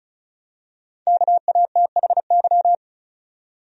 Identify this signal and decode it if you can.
Morse code, heard as a keyed tone, about 36 wpm, 700 Hz. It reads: KATHY
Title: USA Common Names - 35wpm